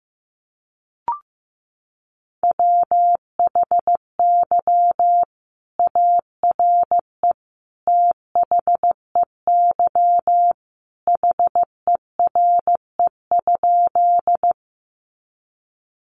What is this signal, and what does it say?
Morse code, heard as a keyed tone, about 15 wpm, 700 Hz. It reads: WHY ARE THEY HERE?
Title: Sentences from Top 200 Words 15wpm